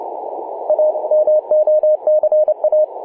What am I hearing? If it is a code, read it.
Morse code, heard as a keyed tone, about 29 wpm, 610 Hz. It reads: A MOCA
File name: posts / oct-19-0756z-vk9xsp-24-901mhz-cw